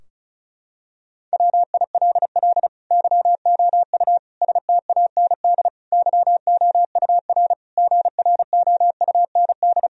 Morse code, heard as a keyed tone, about 36 wpm, 700 Hz. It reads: WILL YOU STAND YOUR GROUND
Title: Sentences from Top 1000 Words - Review Part 2 - 35wpm